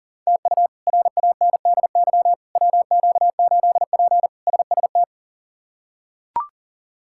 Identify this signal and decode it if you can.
Morse code, heard as a keyed tone, about 40 wpm, 700 Hz. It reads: TU RANDY WQ8P SST
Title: K1USN SST QSO Practice - 40wpm